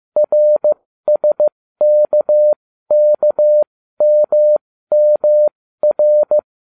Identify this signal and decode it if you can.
Morse code, heard as a keyed tone, about 15 wpm, spaced as for 13 wpm, 600 Hz. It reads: RSKKMMR